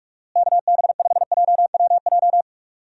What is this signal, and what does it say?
Morse code, heard as a keyed tone, about 46 wpm, 700 Hz. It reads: KB5JWJ
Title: USA Call Signs - 6 Characters - Repeated 2x - 45wpm